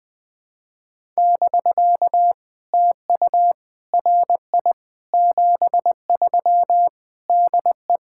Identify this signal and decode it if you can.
Morse code, heard as a keyed tone, about 20 wpm, 700 Hz. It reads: <BK> TU RI 73 DE
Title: POTA QSO Practice - 20wpm